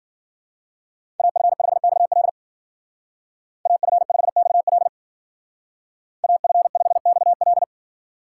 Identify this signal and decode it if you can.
Morse code, heard as a keyed tone, about 49 wpm, 700 Hz. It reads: AF5XL AF5XL AF5XL
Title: Speed Racing - USA 2x2 Call Signs 40wpm